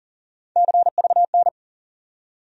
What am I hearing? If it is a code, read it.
Morse code, heard as a keyed tone, about 40 wpm, 700 Hz. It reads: CVN
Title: USA Call Sign Postfixes - 3 Letters 40wpm